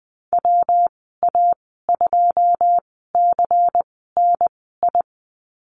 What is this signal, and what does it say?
Morse code, heard as a keyed tone, about 20 wpm, spaced as for 15 wpm, 700 Hz. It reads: WA2CNI
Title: 147 Lesson 24g - Letter y - Call Signs 20wpm